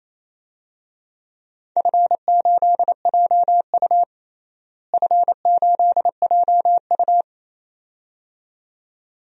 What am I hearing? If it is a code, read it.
Morse code, heard as a keyed tone, about 28 wpm, 700 Hz. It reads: F8JU F8JU
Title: DXCC-100 Call Signs - 4 Characters - Repeated 2x - 28wpm